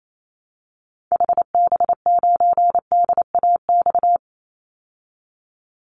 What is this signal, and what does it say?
Morse code, heard as a keyed tone, about 28 wpm, 700 Hz. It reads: HB9DAX
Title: Call Signs Repeated 2x - 28wpm